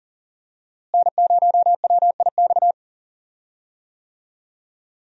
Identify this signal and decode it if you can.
Morse code, heard as a keyed tone, about 40 wpm, 700 Hz. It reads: N0WIX
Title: USA Call Signs - 5 Characters - Repeated 3x - 40wpm